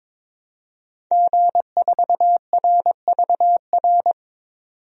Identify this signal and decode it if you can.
Morse code, heard as a keyed tone, about 21 wpm, 700 Hz. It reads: G4RVR
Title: DXCC-100 Call Signs - 5 Characters - 22wpm